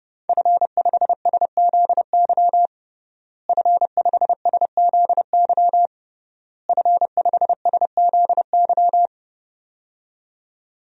Morse code, heard as a keyed tone, about 32 wpm, 700 Hz. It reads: F5SZY F5SZY F5SZY
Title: DXCC-100 Call Signs - 5 Characters - Repeated 3x - 30wpm